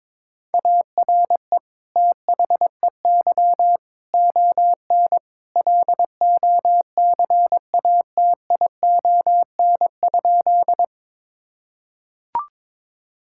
Morse code, heard as a keyed tone, about 22 wpm, 700 Hz. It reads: ARE THEY ON LOCATION?
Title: Sentences from Top 1000 Words - Review Part 6 - 22wpm